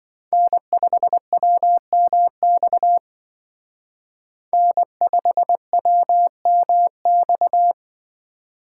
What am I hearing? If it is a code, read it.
Morse code, heard as a keyed tone, about 26 wpm, 700 Hz. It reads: N5WMX N5WMX